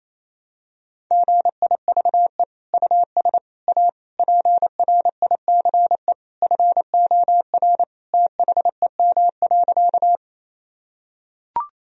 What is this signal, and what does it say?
Morse code, heard as a keyed tone, about 28 wpm, 700 Hz. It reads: GIVE US A PRICE FOR THEM.
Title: Sentences from Top 1000 Words - Review Part 4 - 28wpm